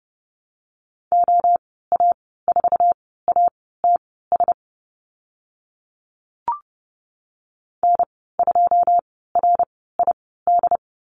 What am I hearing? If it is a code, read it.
Morse code, heard as a keyed tone, about 30 wpm, spaced as for 17 wpm, 700 Hz. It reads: OA4ATS N2RID
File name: Lesson 15g - Number 5 - Call Signs 30wpm